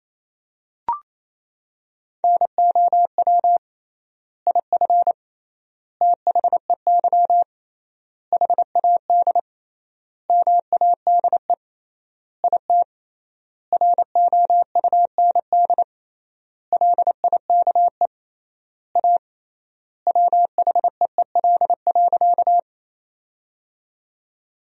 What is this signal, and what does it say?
Morse code, heard as a keyed tone, about 28 wpm, 700 Hz. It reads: NOW IF THEY HAD MADE IT ROUND LIKE A WHEEL.
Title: 3x Word-Spacing - Sentences from Top 500 Words - 28wpm